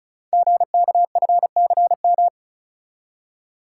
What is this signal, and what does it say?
Morse code, heard as a keyed tone, about 36 wpm, 700 Hz. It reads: GKFCM
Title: Code Groups - 5 Letters - 35wpm